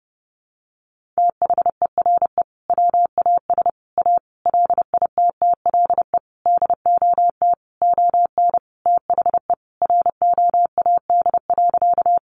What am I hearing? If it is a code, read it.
Morse code, heard as a keyed tone, about 30 wpm, 700 Hz. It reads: THERE WAS A LITTLE DOT ON THE ROAD.
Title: Sentences from Top 1500 Words 30wpm